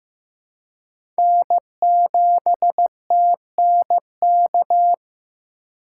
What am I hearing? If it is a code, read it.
Morse code, heard as a keyed tone, about 15 wpm, 700 Hz. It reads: N7TNK
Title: USA Call Signs - 5 Characters - 3x Character Spacing - 15wpm